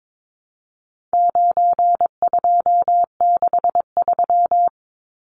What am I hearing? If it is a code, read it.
Morse code, heard as a keyed tone, about 22 wpm, 700 Hz. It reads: 9263
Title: Numbers - 4-Digit 22wpm